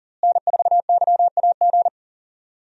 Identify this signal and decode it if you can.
Morse code, heard as a keyed tone, about 40 wpm, 700 Hz. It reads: N4YAG
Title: USA Call Signs - 5 Characters - Repeated 3x - 40wpm